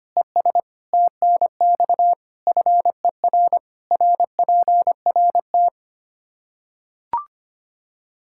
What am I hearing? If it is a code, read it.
Morse code, heard as a keyed tone, about 25 wpm, 700 Hz. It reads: ES TNX FER RPRT
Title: K1USN SST QSO Practice 25wpm